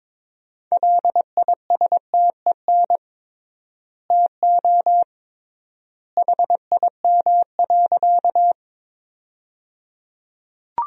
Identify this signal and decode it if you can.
Morse code, heard as a keyed tone, about 23 wpm, 700 Hz. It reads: LISTEN TO HIM.
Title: 3x Word-Spacing - Sentences from Top 400 Words - 22wpm